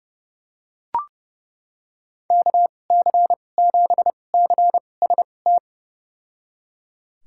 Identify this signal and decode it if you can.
Morse code, heard as a keyed tone, about 32 wpm, spaced as for 22 wpm, 700 Hz. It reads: KC7CST
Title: USA Call Signs - 6 Characters - 2x Character Spacing - 30wpm